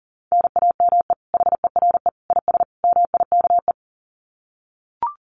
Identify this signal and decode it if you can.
Morse code, heard as a keyed tone, about 40 wpm, 700 Hz. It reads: NAME HERE IS MIKE
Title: Common QSO Phrases 40wpm